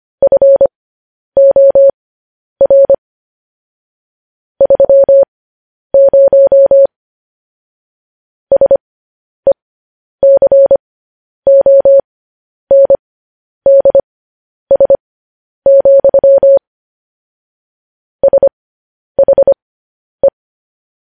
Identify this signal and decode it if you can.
Morse code, heard as a keyed tone, about 25 wpm, spaced as for 10 wpm, 550 Hz. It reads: FOR 30 SECONDS, SHE
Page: news-headlines-in-morse-code-at-10-wpm